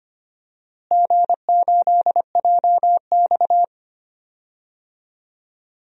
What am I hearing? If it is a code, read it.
Morse code, heard as a keyed tone, about 25 wpm, 700 Hz. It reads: G8JX
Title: DXCC-100 Call Signs - 4 Characters - Repeated 2x - 25wpm